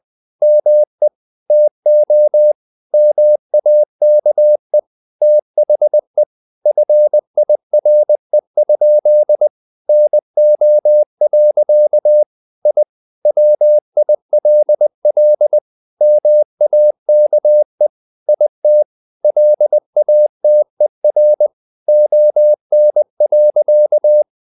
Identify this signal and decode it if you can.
Morse code, heard as a keyed tone, about 20 wpm, 600 Hz. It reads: ME TO MAKE THE FIRE? NO. I WILL MAKE IT LATER ON.
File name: the old man and the sea / the old man and the sea part 1 20wpm